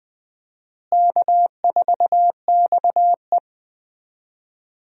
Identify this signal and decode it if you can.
Morse code, heard as a keyed tone, about 20 wpm, 700 Hz. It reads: K4XE
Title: USA Call Signs - 4 Characters - Repeated 2x - 20wpm